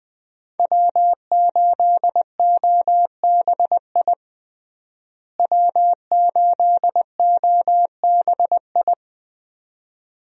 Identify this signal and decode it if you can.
Morse code, heard as a keyed tone, about 20 wpm, 700 Hz. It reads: W8OBI W8OBI
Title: USA Call Signs - 5 Characters - Repeated 2x - 20wpm